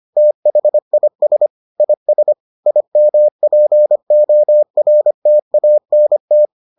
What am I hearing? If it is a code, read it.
Morse code, heard as a keyed tone, about 25 wpm, 600 Hz. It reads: THIS IS IMPORTANT